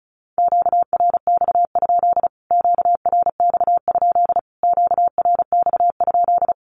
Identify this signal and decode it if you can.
Morse code, heard as a keyed tone, about 34 wpm, 700 Hz. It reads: QRX? QRX? QRX?
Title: Q-Codes Repeated 3x 35wpm